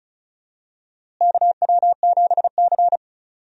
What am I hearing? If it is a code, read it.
Morse code, heard as a keyed tone, about 36 wpm, 700 Hz. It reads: KW7C
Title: USA Call Signs - 4 Characters - Repeated 2x - 35wpm